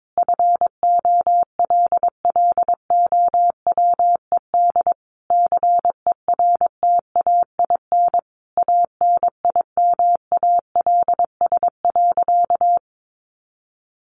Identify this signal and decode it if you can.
Morse code, heard as a keyed tone, about 22 wpm, 700 Hz. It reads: FOLLOWED CERTAIN ANIMALS.
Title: A Sound Of Thunder 22wpm